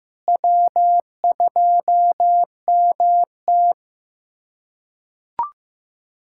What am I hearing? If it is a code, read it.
Morse code, heard as a keyed tone, about 15 wpm, 700 Hz. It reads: W2MT
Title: USA Callsign DE Callsign - 4 Characters - 15wpm